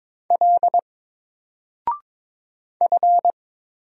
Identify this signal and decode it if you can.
Morse code, heard as a keyed tone, about 22 wpm, 700 Hz. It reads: L F
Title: Single Letter-Number 22wpm